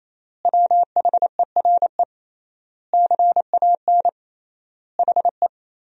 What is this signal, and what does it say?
Morse code, heard as a keyed tone, about 28 wpm, 700 Hz. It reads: WHERE CAN HE
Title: 3x Word-Spacing - Sentences from Top 500 Words - 28wpm